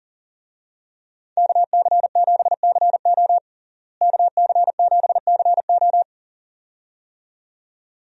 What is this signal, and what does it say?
Morse code, heard as a keyed tone, about 40 wpm, 700 Hz. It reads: KC7CO KC7CO
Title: USA Call Signs - 5 Characters - Repeated 2x - 40wpm